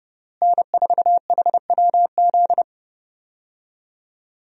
Morse code, heard as a keyed tone, about 32 wpm, 700 Hz. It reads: N4HWZ
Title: USA Call Signs - 5 Characters - Repeated 2x - 30wpm